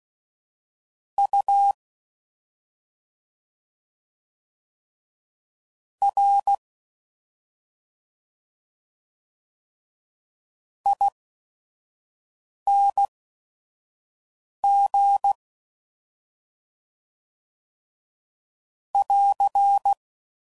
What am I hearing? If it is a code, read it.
Morse code, heard as a keyed tone, about 16 wpm, spaced as for 5 wpm, 800 Hz. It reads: U R ING <AR>